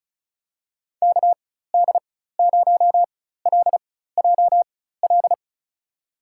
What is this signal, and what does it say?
Morse code, heard as a keyed tone, about 34 wpm, spaced as for 16 wpm, 700 Hz. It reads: KD0LJL
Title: USA Call Signs - 4x Character Spacing - 35wpm